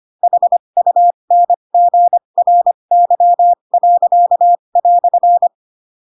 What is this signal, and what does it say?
Morse code, heard as a keyed tone, about 25 wpm, spaced as for 22 wpm, 700 Hz. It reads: HUNGRY."
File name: Pinocchio / Pinocchio ch7 18 25 ik1hge